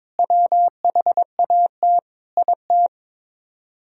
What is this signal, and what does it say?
Morse code, heard as a keyed tone, about 22 wpm, 700 Hz. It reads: WHAT IT
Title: Speed-Racing - Sets of 5 Words 22wpm